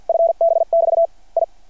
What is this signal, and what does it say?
Morse code, heard as a keyed tone, about 51 wpm, 660 Hz. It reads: FB <BT> I
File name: archived / CRSnet